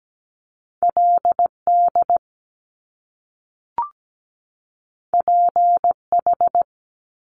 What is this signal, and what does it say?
Morse code, heard as a keyed tone, about 17 wpm, 700 Hz. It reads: LD PH
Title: Top Two Letter Combinations in Words 17wpm